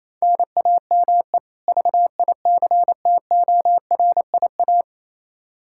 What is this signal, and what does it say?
Morse code, heard as a keyed tone, about 28 wpm, 700 Hz. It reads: NAME VICTORIA
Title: Practice QSO - Exchange 1 - Beginning - 28wpm